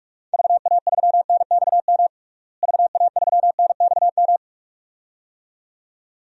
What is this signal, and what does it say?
Morse code, heard as a keyed tone, about 46 wpm, 700 Hz. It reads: VA3NXM VA3NXM
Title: DXCC-100 Call Signs - 6 Characters - Repeated 2x - 45wpm